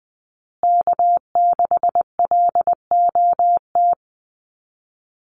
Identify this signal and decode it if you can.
Morse code, heard as a keyed tone, about 20 wpm, 700 Hz. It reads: K6LOT
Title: USA Call Signs Repeated 2x 20wpm